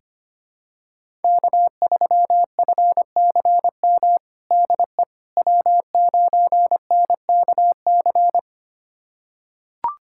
Morse code, heard as a keyed tone, about 25 wpm, 700 Hz. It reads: K3FCM DE W9NKC
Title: USA Callsign DE Callsign - 5 Characters - 25wpm